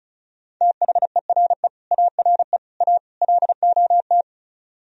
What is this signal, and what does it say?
Morse code, heard as a keyed tone, about 46 wpm, 700 Hz. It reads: THERE ARE A LOT